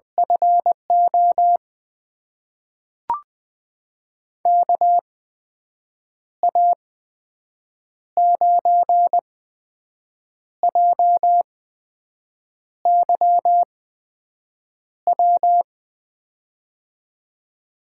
Morse code, heard as a keyed tone, about 20 wpm, spaced as for 6 wpm, 700 Hz. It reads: FO KA9JYW
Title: USA Call Signs - 6 Characters - 8x Character Spacing - 20wpm